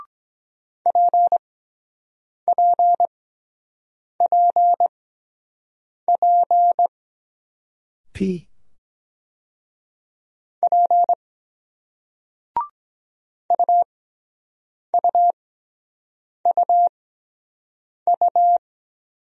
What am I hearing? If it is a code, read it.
Morse code, heard as a keyed tone, about 23 wpm, 700 Hz. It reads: P P P P P U U U U